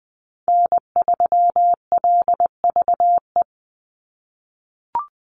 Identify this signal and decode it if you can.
Morse code, heard as a keyed tone, about 20 wpm, 700 Hz. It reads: N3LVE
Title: USA Call Signs 20wpm